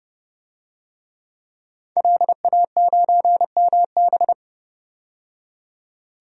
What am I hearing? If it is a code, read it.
Morse code, heard as a keyed tone, about 31 wpm, 690 Hz. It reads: LA9MB
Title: DXCC-100 Call Signs - 5 Characters - 30wpm